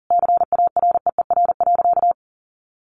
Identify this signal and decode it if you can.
Morse code, heard as a keyed tone, about 37 wpm, 700 Hz. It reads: CAREER.
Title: Sentences from Top 1200 Words 40wpm